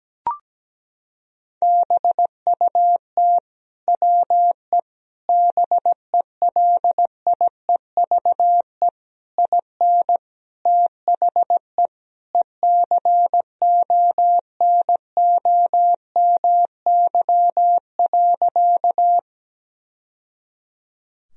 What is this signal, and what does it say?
Morse code, heard as a keyed tone, about 17 wpm, 700 Hz. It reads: BUT WE BELIEVE IN THE ECONOMY.
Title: Sentences from Top 1000 Words - Review Part 3 - 17wpm